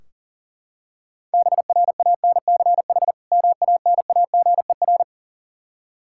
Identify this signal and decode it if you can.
Morse code, heard as a keyed tone, about 37 wpm, 700 Hz. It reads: BRANCH MANAGER
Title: Practice QSO - Exchange 3 - More Info - 40wpm